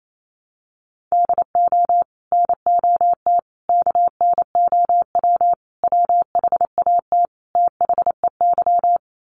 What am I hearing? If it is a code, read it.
Morse code, heard as a keyed tone, about 28 wpm, 700 Hz. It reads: DO NOT KNOW WHAT THEY
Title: Sets of 5 Words 28wpm